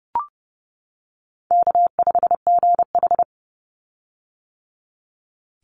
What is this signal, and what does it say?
Morse code, heard as a keyed tone, about 30 wpm, 700 Hz. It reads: K5GH
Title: Easy Call Signs - 30wpm